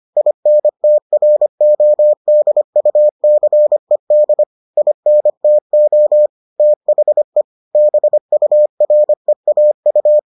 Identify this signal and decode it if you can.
Morse code, heard as a keyed tone, about 25 wpm, 600 Hz. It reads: INTRODUCED INTO THE BUREAU